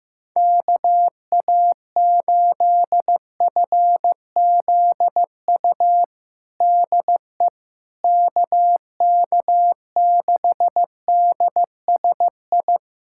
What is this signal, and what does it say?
Morse code, heard as a keyed tone, about 15 wpm, 700 Hz. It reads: KA8FZU DE KK6DSI
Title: USA Callsign DE Callsign - 6 Characters - 15wpm